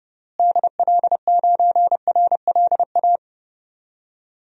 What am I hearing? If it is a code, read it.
Morse code, heard as a keyed tone, about 30 wpm, 700 Hz. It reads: DL9RLA
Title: DXCC-100 Call Signs - 6 Characters - Repeated 3x - 30wpm